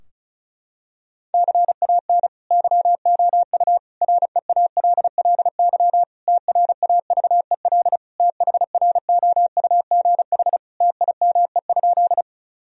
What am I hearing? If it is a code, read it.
Morse code, heard as a keyed tone, about 36 wpm, 700 Hz. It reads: CAN YOU REALLY TRAVEL THROUGH TIME?
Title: Sentences from Top 1000 Words - Review Part 2 - 35wpm